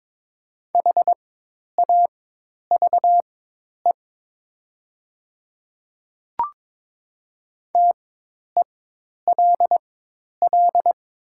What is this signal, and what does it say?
Morse code, heard as a keyed tone, about 22 wpm, spaced as for 10 wpm, 700 Hz. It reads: HAVE TELL
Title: 4x Character Spacing - Top 300 Words - 22wpm